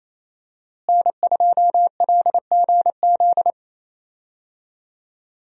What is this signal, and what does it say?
Morse code, heard as a keyed tone, about 27 wpm, 700 Hz. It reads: N2LGZ
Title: USA Call Signs - 5 Characters - Repeated 3x - 28wpm